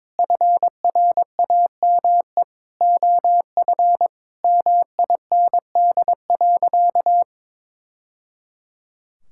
Morse code, heard as a keyed tone, about 22 wpm, 700 Hz. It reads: FRAME OF MIND.
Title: Sentences from Top 1000 Words - Review Part 6 - 22wpm